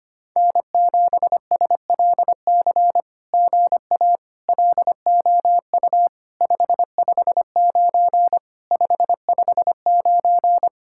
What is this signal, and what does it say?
Morse code, heard as a keyed tone, about 24 wpm, 700 Hz. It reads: N7SLC GA LOU 559 559
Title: SOTA QSO Practice 25wpm